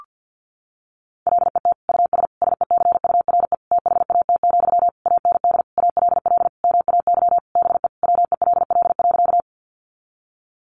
Morse code, heard as a keyed tone, about 50 wpm, 700 Hz. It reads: LET US SECURE THAT, AND ALL MAY BE WELL.
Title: Sentences from Top 1200 Words 50wpm